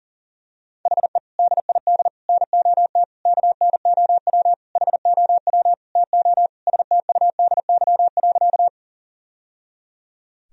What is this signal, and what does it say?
Morse code, heard as a keyed tone, about 39 wpm, 700 Hz. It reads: HE DID NOT KNOW HOW TO STUDY.
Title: Sentences from Top 1000 Words - Review Part 1 - 40wpm